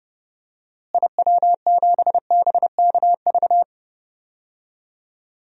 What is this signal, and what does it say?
Morse code, heard as a keyed tone, about 30 wpm, 700 Hz. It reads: IW7BKV
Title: DXCC-100 Call Signs - 6 Characters - Repeated 3x - 30wpm